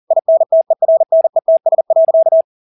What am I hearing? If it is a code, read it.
Morse code, heard as a keyed tone, about 40 wpm, 650 Hz. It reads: INTERNETS.
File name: dk5ke / Motto 200BpM